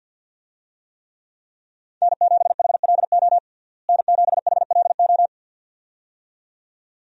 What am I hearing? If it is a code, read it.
Morse code, heard as a keyed tone, about 53 wpm, 700 Hz. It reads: N7HLO N7HLO
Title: USA Call Signs - 5 Characters - Repeated 2x - 50wpm